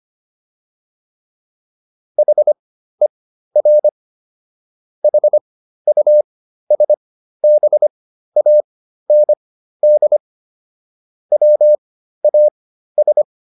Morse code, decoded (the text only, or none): HER HUSBAND WAS